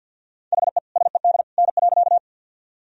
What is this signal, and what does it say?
Morse code, heard as a keyed tone, about 51 wpm, 700 Hz. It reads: HE SED N.
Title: Sentences from Top 100 Words and Abbreviations 50wpm